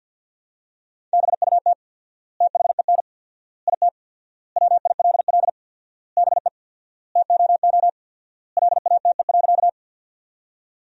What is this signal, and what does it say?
Morse code, heard as a keyed tone, about 50 wpm, 700 Hz. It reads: BUT THEN IT WILL BE TOO LATE.
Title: 4x Word-Spacing - Sentences from Top 300 Words - 50wpm